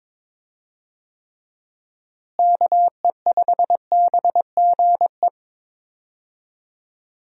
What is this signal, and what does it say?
Morse code, heard as a keyed tone, about 22 wpm, 700 Hz. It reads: KE5BGE
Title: USA Call Signs - 6 Characters - Repeated 2x - 22wpm